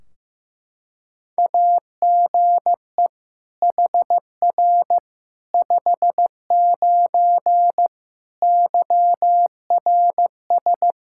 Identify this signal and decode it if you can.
Morse code, heard as a keyed tone, about 15 wpm, 700 Hz. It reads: AGE HR 59 YRS
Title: K1USN SST QSO Practice 15wpm